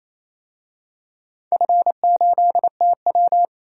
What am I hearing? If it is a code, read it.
Morse code, heard as a keyed tone, about 27 wpm, 690 Hz. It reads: F8TW
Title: DXCC-100 Call Signs - 4 Characters - 28wpm